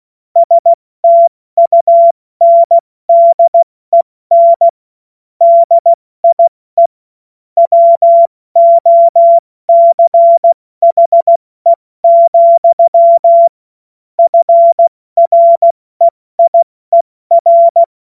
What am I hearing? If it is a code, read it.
Morse code, heard as a keyed tone, about 16 wpm, 680 Hz. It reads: STUNDEN DIE WOCHE, FREIER
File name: Morsekurs / Witze / Witz 29. November 2022 70 F80